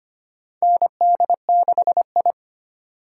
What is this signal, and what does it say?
Morse code, heard as a keyed tone, about 25 wpm, 700 Hz. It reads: ND6I
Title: USA Call Signs - 4 Characters - 4x Character Spacing - 25wpm